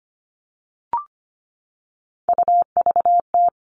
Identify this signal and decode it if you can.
Morse code, heard as a keyed tone, about 25 wpm, 700 Hz. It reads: UVT
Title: All Cut Numbers - 3-Digit 25wpm